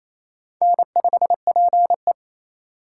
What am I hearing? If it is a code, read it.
Morse code, heard as a keyed tone, about 33 wpm, 700 Hz. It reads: N5PE
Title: Speed Racing - USA Call Signs 28wpm